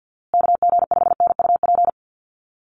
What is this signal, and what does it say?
Morse code, heard as a keyed tone, about 50 wpm, 700 Hz. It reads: KG5NUP
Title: USA Call Signs Repeated 2x 50wpm